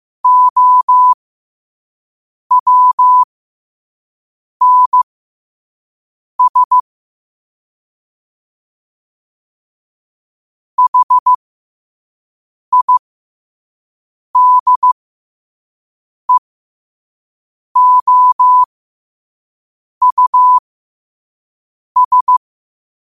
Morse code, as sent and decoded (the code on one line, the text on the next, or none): --- .-- -. ... / .... .. -.. --- ..- ...
OWNS HIDEOUS